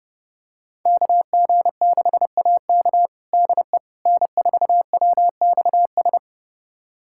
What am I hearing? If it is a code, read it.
Morse code, heard as a keyed tone, about 30 wpm, 700 Hz. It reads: KG6AK DE N4WXS
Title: USA Callsign DE Callsign - 5 Characters - 30wpm